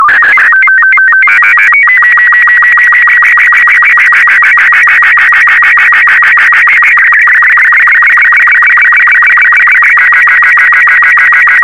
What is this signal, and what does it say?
SSTV (slow-scan television), Robot 36.
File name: sstv / R36-VK4MSL